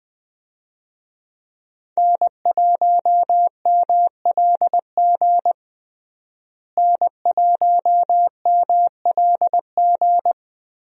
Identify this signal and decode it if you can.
Morse code, heard as a keyed tone, about 20 wpm, 700 Hz. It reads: N1MLG N1MLG
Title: USA Call Signs - 5 Characters - Repeated 2x - 20wpm